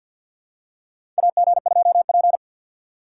Morse code, heard as a keyed tone, about 48 wpm, 700 Hz. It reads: AG2P
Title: USA Call Signs - 4 Characters - 6x Character Spacing - 50wpm